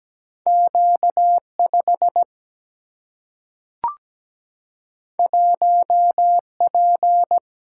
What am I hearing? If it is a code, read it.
Morse code, heard as a keyed tone, about 17 wpm, 700 Hz. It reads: Q5 1P
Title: Single-Letter Single-Number 17wpm